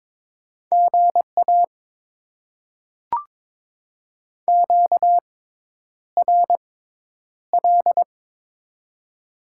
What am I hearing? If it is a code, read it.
Morse code, heard as a keyed tone, about 21 wpm, spaced as for 10 wpm, 700 Hz. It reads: GA QRL
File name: Top 90 QSO Elements - 6x Character Spacing - 22wpm